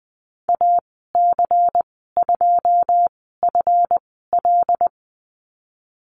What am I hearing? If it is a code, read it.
Morse code, heard as a keyed tone, about 20 wpm, spaced as for 15 wpm, 700 Hz. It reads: AC2FL
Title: 154 Lesson 25g - Letter p - Call Signs 20wpm